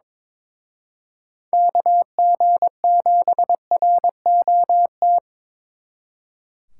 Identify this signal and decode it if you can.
Morse code, heard as a keyed tone, about 27 wpm, 700 Hz. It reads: KG7ROT
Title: Speed Racing - USA 2x3 Call Signs 22wpm